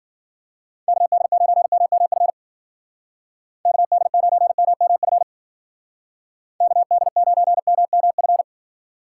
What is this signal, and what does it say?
Morse code, heard as a keyed tone, about 55 wpm, 700 Hz. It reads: KD9MMF KD9MMF KD9MMF